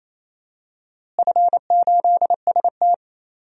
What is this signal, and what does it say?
Morse code, heard as a keyed tone, about 28 wpm, 700 Hz. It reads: F8ST
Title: DXCC-100 Call Signs - 4 Characters - Repeated 3x - 28wpm